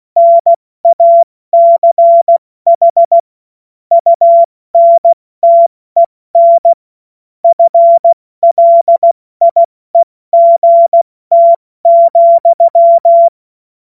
Morse code, heard as a keyed tone, about 16 wpm, spaced as for 14 wpm, 680 Hz. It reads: NACH UNTEN FLIEGT,
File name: Morsekurs / Witze / Witz 31.Oktober 2023 70 F80